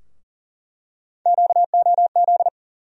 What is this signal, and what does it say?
Morse code, heard as a keyed tone, about 40 wpm, 700 Hz. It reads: QOZ
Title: USA Call Sign Postfixes - 3 Letters 40wpm